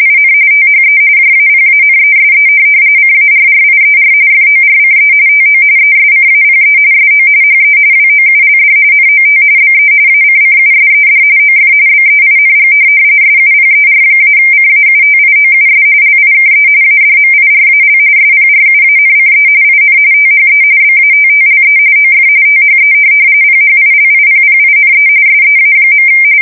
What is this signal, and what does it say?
RTTY, about 45.45 baud. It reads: IN THE JAPANESE CARRIER FORCE TO ESCAPE SO HE WOULD NOT BE ACCUSED OF BEING SUPER-CAUTIOUS LIKE ADMIRAL SPRUANCE IN THE BATTLE OF THE PHILIPPINE SEA.
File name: fsk170 / USS-New-Jersey